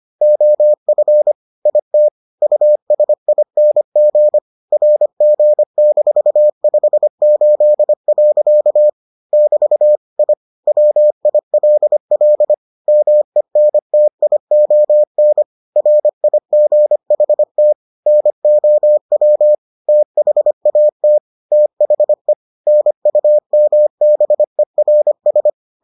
Morse code, heard as a keyed tone, about 25 wpm, 600 Hz. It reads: OF IT USING RG-58. <BT> I WILL MENTION RIGHT NOW THAT THE NUMBERS